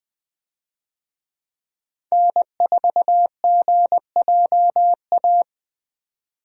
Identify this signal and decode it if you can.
Morse code, heard as a keyed tone, about 20 wpm, 700 Hz. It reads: N4GJA
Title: USA Call Signs - 5 Characters - 20wpm